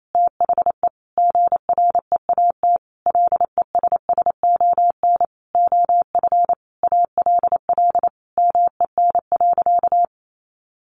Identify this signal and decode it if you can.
Morse code, heard as a keyed tone, about 28 wpm, 700 Hz. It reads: THE GREAT LESSON OF ALL MEN.
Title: Sentences from Top 1200 Words 28wpm